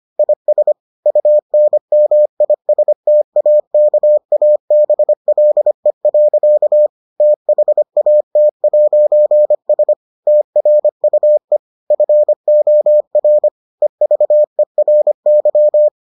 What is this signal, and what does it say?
Morse code, heard as a keyed tone, about 25 wpm, 600 Hz. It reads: IS UNMISTAKABLE. THAT'S TRUE FOR EVERY